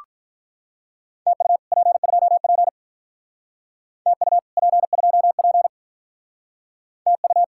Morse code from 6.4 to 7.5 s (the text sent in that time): TU